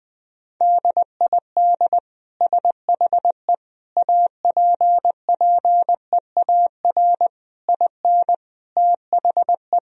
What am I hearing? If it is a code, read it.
Morse code, heard as a keyed tone, about 21 wpm, 700 Hz. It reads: DID SHE APPEAR IN THE
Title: Sentences from Top 1000 Words - Review Part 4 - 20wpm